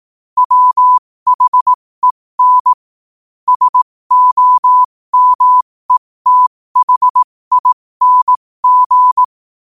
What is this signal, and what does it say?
Morse code, heard as a keyed tone, about 18 wpm, spaced as for 15 wpm, 1000 Hz. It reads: WHEN SOMETHING
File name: qotdfiles / 15wpm